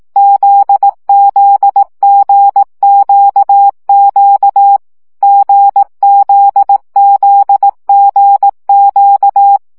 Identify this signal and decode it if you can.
Morse code, heard as a keyed tone, about 18 wpm, 800 Hz. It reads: ZZGQQ GZZGQ
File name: CW gyakorlas g z q 15 baud